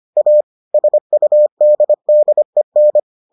Morse code, heard as a keyed tone, about 25 wpm, 600 Hz. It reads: A SUDDEN